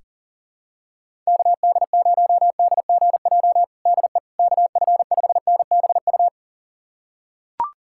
Morse code, heard as a keyed tone, about 40 wpm, 700 Hz. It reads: KD0DGJ DE KF5NBU
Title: USA Callsign DE Callsign - 6 Characters - 40wpm